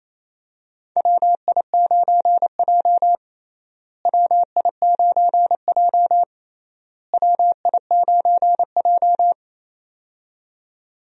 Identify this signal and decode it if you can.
Morse code, heard as a keyed tone, about 28 wpm, 700 Hz. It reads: WI9J WI9J WI9J
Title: USA Call Signs - 4 Characters - Repeated 3x - 28wpm